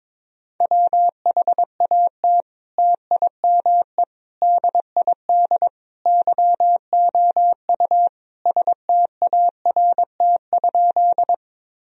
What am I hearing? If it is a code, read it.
Morse code, heard as a keyed tone, about 22 wpm, 700 Hz. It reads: WHAT TIME DID YOU START?
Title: Sentences from Top 1000 Words - Review Part 1 - 22wpm